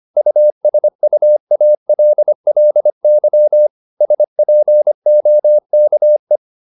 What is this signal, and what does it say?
Morse code, heard as a keyed tone, about 25 wpm, 600 Hz. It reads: USUALLY SPOKE
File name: the old man and the sea part 1 25wpm